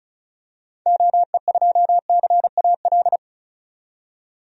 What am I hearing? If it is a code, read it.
Morse code, heard as a keyed tone, about 34 wpm, 700 Hz. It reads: OE2CAL
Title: DXCC-100 Call Signs - 6 Characters - 35wpm